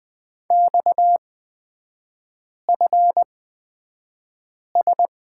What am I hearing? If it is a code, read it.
Morse code, heard as a keyed tone, about 20 wpm, spaced as for 5 wpm, 700 Hz. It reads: XFS
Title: 236 Lesson 38e - Letter q - Three Characters 20wpm